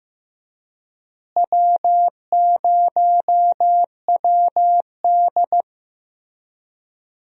Morse code, heard as a keyed tone, about 15 wpm, 700 Hz. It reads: W0WD
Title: USA Call Signs - 4 Characters - Repeated 2x - 15wpm